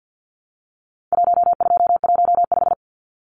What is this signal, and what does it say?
Morse code, heard as a keyed tone, about 50 wpm, 700 Hz. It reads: 1215
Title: Numbers - 4-Digit 50wpm